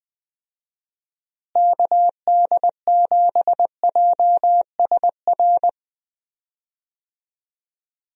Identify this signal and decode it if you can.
Morse code, heard as a keyed tone, about 20 wpm, 700 Hz. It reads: KD7JSR
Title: USA Call Signs - 6 Characters - Repeated 3x - 20wpm